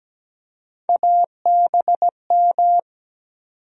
Morse code, heard as a keyed tone, about 17 wpm, 700 Hz. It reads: ABM
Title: USA Call Sign Postfixes - 3 Letters 17wpm